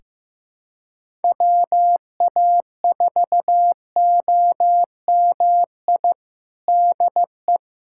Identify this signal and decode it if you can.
Morse code, heard as a keyed tone, about 15 wpm, 700 Hz. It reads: WA4OMI DE
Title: USA Callsign DE Callsign - 6 Characters - 15wpm